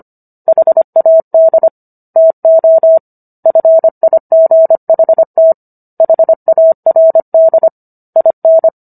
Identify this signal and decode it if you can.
Morse code, heard as a keyed tone, about 25 wpm, 650 Hz. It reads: HAD TO FIGHT HARD IN
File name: Book One / 16-Chapter 16 - 25 WPM